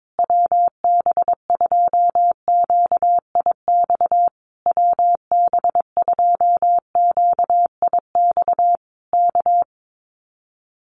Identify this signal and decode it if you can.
Morse code, heard as a keyed tone, about 22 wpm, 700 Hz. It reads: WB2QIX WB2QIX K
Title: Common QSO Phrases Encore 22wpm